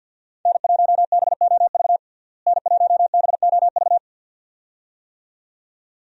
Morse code, heard as a keyed tone, about 49 wpm, 700 Hz. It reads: N1BOV N1BOV
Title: USA Call Signs - 5 Characters - Repeated 2x - 50wpm